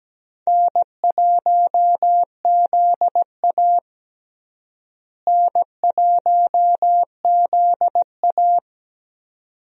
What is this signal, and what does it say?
Morse code, heard as a keyed tone, about 17 wpm, 700 Hz. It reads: N1ZA N1ZA
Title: USA Call Signs - 4 Characters - Repeated 2x - 17wpm